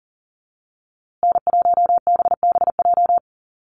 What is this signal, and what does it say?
Morse code, heard as a keyed tone, about 40 wpm, 700 Hz. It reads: N1BBJ
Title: USA Call Signs Repeated 3x 40wpm